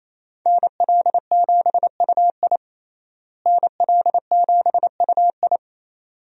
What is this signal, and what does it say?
Morse code, heard as a keyed tone, about 28 wpm, 700 Hz. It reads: NL7UI NL7UI
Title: USA Call Signs - 5 Characters - Repeated 2x - 28wpm